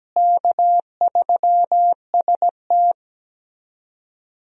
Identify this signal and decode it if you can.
Morse code, heard as a keyed tone, about 17 wpm, 700 Hz. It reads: K3ST
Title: USA Call Signs - 4 Characters - Repeated 2x - 17wpm